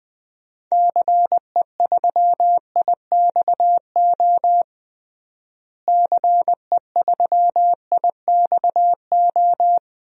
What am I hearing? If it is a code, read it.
Morse code, heard as a keyed tone, about 20 wpm, 700 Hz. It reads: CE3IXO CE3IXO
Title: DXCC-100 Call Signs - 6 Characters - Repeated 2x - 20wpm